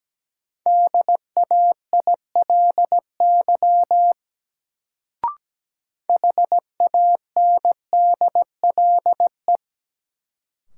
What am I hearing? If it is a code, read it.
Morse code, heard as a keyed tone, about 17 wpm, 700 Hz. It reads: DAILY HANDLE
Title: Sentences from Top 1100 Words - New Word Drill Alt - 17wpm